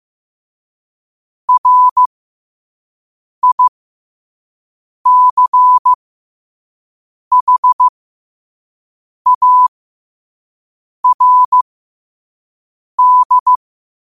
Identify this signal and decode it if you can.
Morse code, heard as a keyed tone, about 15 wpm, spaced as for 5 wpm, 1000 Hz. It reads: RICHARD
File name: qotdfiles / 5wpm